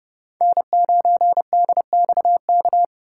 Morse code, heard as a keyed tone, about 30 wpm, 690 Hz. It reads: N9DXK
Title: USA Call Signs - 5 Characters - 6x Character Spacing - 30wpm